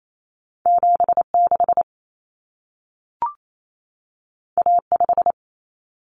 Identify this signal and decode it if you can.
Morse code, heard as a keyed tone, about 28 wpm, 700 Hz. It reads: 76 A5
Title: Common Cut Numbers - 2-Digit 28wpm